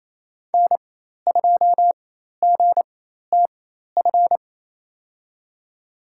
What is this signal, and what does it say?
Morse code, heard as a keyed tone, about 28 wpm, spaced as for 13 wpm, 700 Hz. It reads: N2GTF
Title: USA Call Signs - 5 Characters - 4x Character Spacing - 28wpm